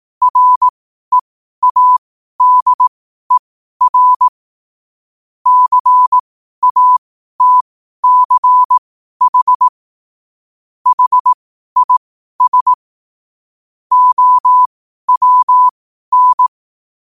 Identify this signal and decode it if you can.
Morse code, heard as a keyed tone, about 18 wpm, spaced as for 12 wpm, 1000 Hz. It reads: READER CATCH HIS OWN
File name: qotdfiles / 12wpm